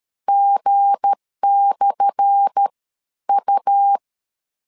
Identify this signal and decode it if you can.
Morse code, heard as a keyed tone, about 13 wpm, 800 Hz. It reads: G/ U
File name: LEZIONI / 10 SIGNS / 10D COMA 12wpm 2544